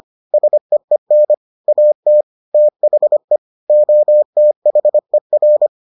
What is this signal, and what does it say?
Morse code, heard as a keyed tone, about 25 wpm, 600 Hz. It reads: SEEN AT THE OTHER